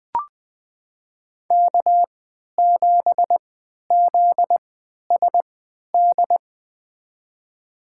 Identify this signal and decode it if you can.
Morse code, heard as a keyed tone, about 20 wpm, spaced as for 11 wpm, 700 Hz. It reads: K7ZSD
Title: USA Call Signs - 5 Characters - 3x Character Spacing - 20wpm